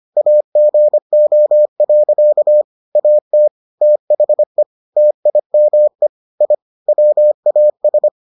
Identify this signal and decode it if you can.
Morse code, heard as a keyed tone, about 25 wpm, 600 Hz. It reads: AGO. AT THE TIME I WAS